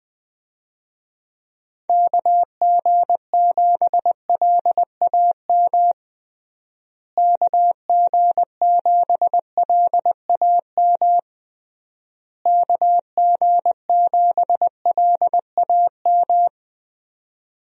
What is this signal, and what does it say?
Morse code, heard as a keyed tone, about 20 wpm, 700 Hz. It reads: KG7LAM KG7LAM KG7LAM
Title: USA Call Signs - 6 Characters - Repeated 3x - 20wpm